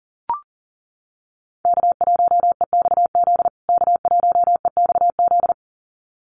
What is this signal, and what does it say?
Morse code, heard as a keyed tone, about 40 wpm, 700 Hz. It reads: K1EXZ K1EXZ
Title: USA Call Signs Repeated 2x 40wpm